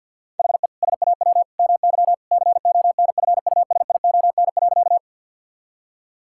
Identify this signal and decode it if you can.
Morse code, heard as a keyed tone, about 50 wpm, 700 Hz. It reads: HE SAW MY CONFUSION.
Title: Sentences from Top 1600 Words 50wpm